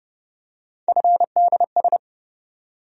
Morse code, heard as a keyed tone, about 30 wpm, 700 Hz. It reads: FDS
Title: USA Call Sign Postfixes - 3 Letters 30wpm